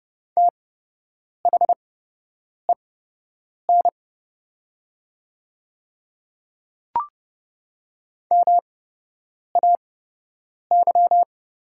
Morse code, heard as a keyed tone, about 29 wpm, spaced as for 8 wpm, 700 Hz. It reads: THEN MAY
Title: 8x Character Spacing - Top 200 Words - 30wpm